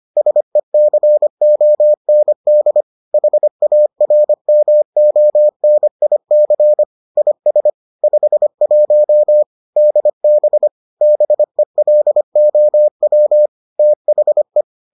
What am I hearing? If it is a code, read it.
Morse code, heard as a keyed tone, about 25 wpm, 600 Hz. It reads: SECOND HARMONIC IS 51 DB BELOW THE